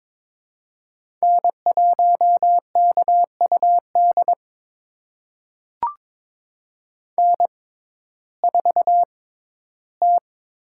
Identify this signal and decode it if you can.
Morse code, heard as a keyed tone, about 23 wpm, spaced as for 14 wpm, 700 Hz. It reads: N1KUD N4T
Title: USA Call Signs - 5 Characters - 6x Character Spacing - 22wpm